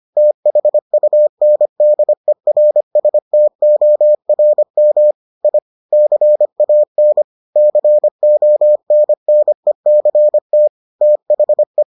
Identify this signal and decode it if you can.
Morse code, heard as a keyed tone, about 25 wpm, 600 Hz. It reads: THUNDERSTORM I CAN CONNECT THE